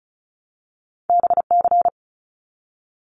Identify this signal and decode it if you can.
Morse code, heard as a keyed tone, about 36 wpm, 700 Hz. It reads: BC
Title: Canadian Provinces and Territories - Repeated 3x - 35wpm